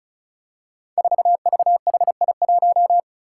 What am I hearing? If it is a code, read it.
Morse code, heard as a keyed tone, about 36 wpm, 700 Hz. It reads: 4VHI1
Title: Five Challenging Letter-Number Combinations 35wpm